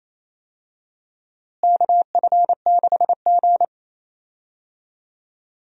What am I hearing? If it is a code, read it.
Morse code, heard as a keyed tone, about 28 wpm, 700 Hz. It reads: KF6G
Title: USA Call Signs - 4 Characters - Repeated 3x - 28wpm